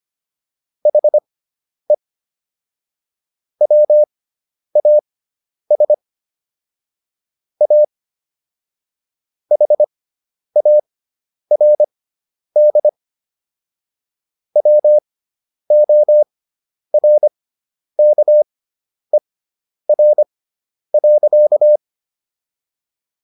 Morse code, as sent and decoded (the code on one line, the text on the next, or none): .... / .-- .- ... / .- / .... .- .-. -.. / .-- --- .-. -.- . .-. .-.-.-
HE WAS A HARD WORKER.